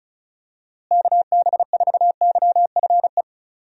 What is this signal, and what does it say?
Morse code, heard as a keyed tone, about 36 wpm, 700 Hz. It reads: KB4YFE
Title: USA Call Signs - 6 Characters - Repeated 2x - 35wpm